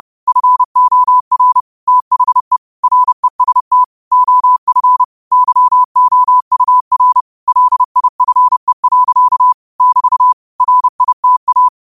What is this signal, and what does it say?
Morse code, heard as a keyed tone, about 30 wpm, 1000 Hz. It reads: FOR THE REST OF YOUR LIFE. <BT> RITA
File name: qotdfiles / 30wpm